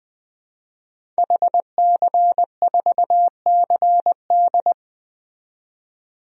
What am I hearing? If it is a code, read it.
Morse code, heard as a keyed tone, about 20 wpm, 700 Hz. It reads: HC4CD
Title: DXCC-100 Call Signs - 5 Characters - 20wpm